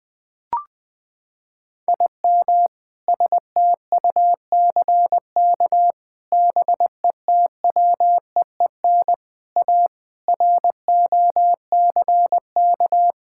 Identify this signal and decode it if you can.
Morse code, heard as a keyed tone, about 20 wpm, 700 Hz. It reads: IM STUCK BETWEEN A ROCK
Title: Sentences with 7-Word Phrases - 20wpm